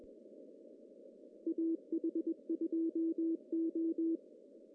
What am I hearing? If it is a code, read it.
Morse code, heard as a keyed tone, about 22 wpm, 330 Hz. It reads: AH2O
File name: CW / cqww-cw-k5tr